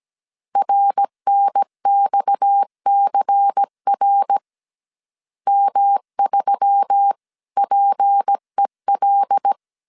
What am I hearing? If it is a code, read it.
Morse code, heard as a keyed tone, about 17 wpm, 800 Hz. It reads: RNXCR M3PEL
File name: 16wpm